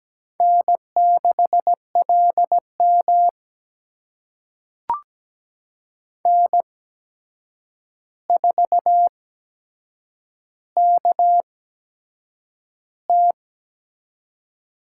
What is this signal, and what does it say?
Morse code, heard as a keyed tone, about 17 wpm, spaced as for 7 wpm, 700 Hz. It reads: N6LM N4KT
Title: USA Call Signs - 4 Characters - 8x Character Spacing - 17wpm